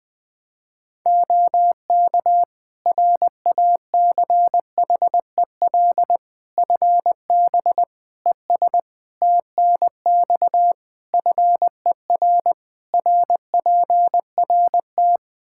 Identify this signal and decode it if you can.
Morse code, heard as a keyed tone, about 20 wpm, 700 Hz. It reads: OK RACHEL FB ES TNX FER RPRT
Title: K1USN SST QSO Practice 20wpm